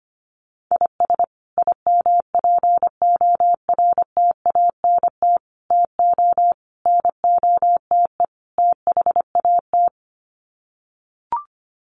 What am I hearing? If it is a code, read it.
Morse code, heard as a keyed tone, about 25 wpm, 700 Hz. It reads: IS IMPORTANT TO NOTE THAT
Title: Sets of 5 Words 25wpm